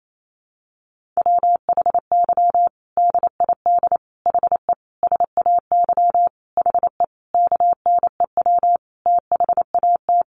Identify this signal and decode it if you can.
Morse code, heard as a keyed tone, about 28 wpm, 700 Hz. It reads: WHY DID HE SAY HE KNEW THAT
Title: Sentences from Top 1500 Words 28wpm